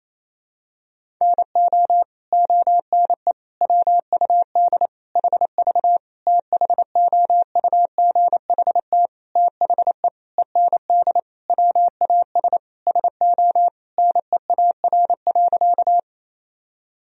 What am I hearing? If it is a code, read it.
Morse code, heard as a keyed tone, about 28 wpm, 700 Hz. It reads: NO ONE WUD HV THOUGHT THE END WAS SO NEAR.
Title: Sentences from Top 300 Words and Abbreviations 28wpm